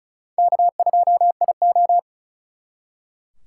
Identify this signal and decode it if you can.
Morse code, heard as a keyed tone, about 45 wpm, 700 Hz. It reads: K2IO